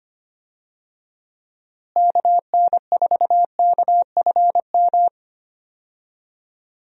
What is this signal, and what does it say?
Morse code, heard as a keyed tone, about 25 wpm, 700 Hz. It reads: KN4KFM
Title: USA Call Signs - 6 Characters - Repeated 3x - 25wpm